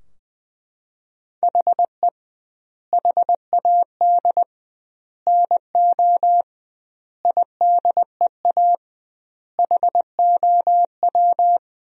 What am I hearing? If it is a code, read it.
Morse code, heard as a keyed tone, about 20 wpm, 700 Hz. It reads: HE HAD NO IDEA HOW
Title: 2x Word-Spacing - Sentences from Top 300 Words - 20wpm